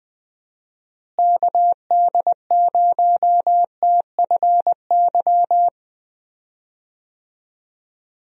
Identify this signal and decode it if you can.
Morse code, heard as a keyed tone, about 20 wpm, 700 Hz. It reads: KD0TFY
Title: USA Call Signs - 6 Characters - Repeated 2x - 20wpm